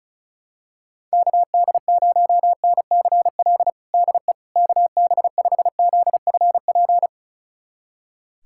Morse code, heard as a keyed tone, about 36 wpm, 700 Hz. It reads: KD0NCL DE KB5ZFP
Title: USA Callsign DE Callsign - 6 Characters - 35wpm